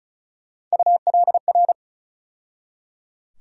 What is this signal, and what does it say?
Morse code, heard as a keyed tone, about 36 wpm, 700 Hz. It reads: ULR
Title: USA Call Sign Postfixes - 3 Letters 35wpm